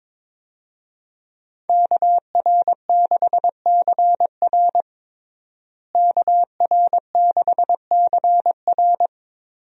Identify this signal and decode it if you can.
Morse code, heard as a keyed tone, about 22 wpm, 700 Hz. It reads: KR6CR KR6CR
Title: USA Call Signs - 5 Characters - Repeated 3x - 22wpm